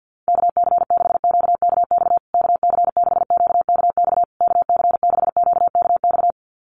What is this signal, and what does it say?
Morse code, heard as a keyed tone, about 50 wpm, 700 Hz. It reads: KC6QKX KC6QKX KC6QKX
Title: USA Call Signs Repeated 3x 50wpm